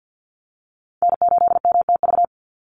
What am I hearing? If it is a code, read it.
Morse code, heard as a keyed tone, about 50 wpm, 700 Hz. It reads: N8MTV
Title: Call Signs - 50wpm